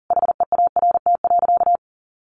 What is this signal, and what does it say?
Morse code, heard as a keyed tone, about 40 wpm, 700 Hz. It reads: HEART.
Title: Sentences from Top 1400 Words 40wpm